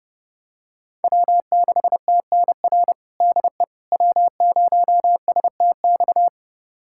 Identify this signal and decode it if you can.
Morse code, heard as a keyed tone, about 32 wpm, 700 Hz. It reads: W6TNR DE W0STX
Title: USA Callsign DE Callsign - 5 Characters - 30wpm